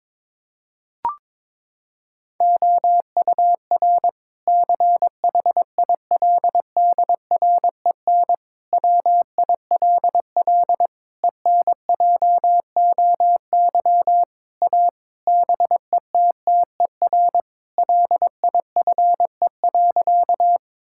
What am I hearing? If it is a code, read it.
Morse code, heard as a keyed tone, about 22 wpm, 700 Hz. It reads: OUR CHILDREN WILL ENJOY A BETTER LIFE.
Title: Sentences from Top 1000 Words - Review Part 5 - 22wpm